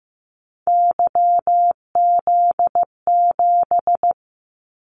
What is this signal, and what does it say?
Morse code, heard as a keyed tone, about 15 wpm, 700 Hz. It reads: YZ7
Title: Callsign Prefixes for Top 25 DXCC - 15wpm